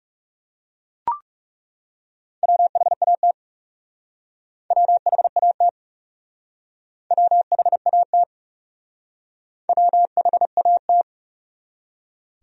Morse code, heard as a keyed tone, about 38 wpm, 700 Hz. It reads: WHAT WHAT WHAT WHAT